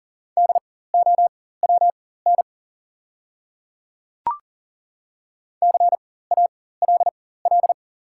Morse code, heard as a keyed tone, about 40 wpm, spaced as for 19 wpm, 690 Hz. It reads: DOWN CALL
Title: 4x Character Spacing - Top 100 Words - 40wpm